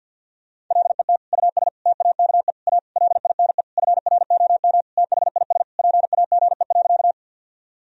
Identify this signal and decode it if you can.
Morse code, heard as a keyed tone, about 47 wpm, 700 Hz. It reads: LET US TAKE A LINE FROM THIS PAGE.
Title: Sentences from Top 1000 Words - Review Part 1 - 50wpm